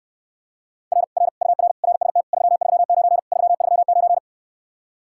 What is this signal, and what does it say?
Morse code, heard as a keyed tone, about 69 wpm, 700 Hz. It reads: R R UR RST 339 339